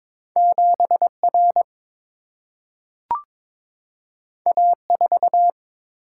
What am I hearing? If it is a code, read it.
Morse code, heard as a keyed tone, about 22 wpm, 700 Hz. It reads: 7R A4
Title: Single-Letter Single-Number 22wpm